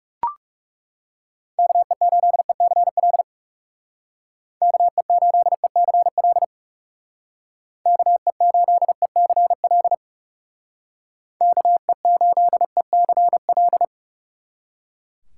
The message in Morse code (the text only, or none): KE8ECL KE8ECL KE8ECL KE8ECL